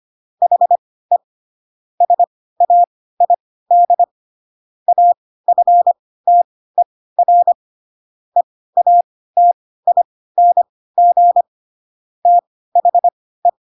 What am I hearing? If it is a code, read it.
Morse code, heard as a keyed tone, about 25 wpm, spaced as for 16 wpm, 700 Hz. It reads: HE SAID AFTER EATING THE
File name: Pinocchio / Pinocchio ch7 10 25 ik1hge